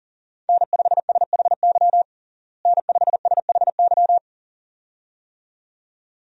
Morse code, heard as a keyed tone, about 37 wpm, 700 Hz. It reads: N5SHY N5SHY
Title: USA Call Signs - 5 Characters - Repeated 2x - 40wpm